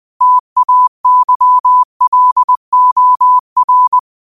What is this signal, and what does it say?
Morse code, heard as a keyed tone, about 20 wpm, 1000 Hz. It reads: TAYLOR
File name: qotdfiles / 20wpm